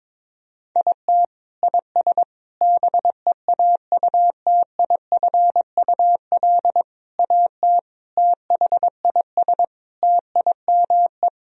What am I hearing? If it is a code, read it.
Morse code, heard as a keyed tone, about 22 wpm, 700 Hz. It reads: IT IS BEAUTIFUL AT THIS TIME
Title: Sentences from Top 1000 Words - Review Part 1 - 22wpm